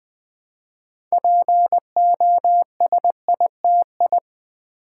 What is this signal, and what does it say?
Morse code, heard as a keyed tone, about 20 wpm, 700 Hz. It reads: POSITI
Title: Build A Word - 2 Letters At A Time - 8-Letter Words - 20wpm